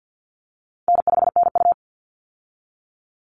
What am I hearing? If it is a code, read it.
Morse code, heard as a keyed tone, about 50 wpm, 700 Hz. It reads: N5NU
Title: Easy Call Signs Repeated 3x - 50wpm